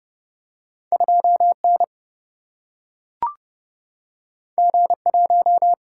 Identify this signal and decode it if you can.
Morse code, heard as a keyed tone, about 29 wpm, 700 Hz. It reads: 2N G1
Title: Single-Letter Single-Number 30wpm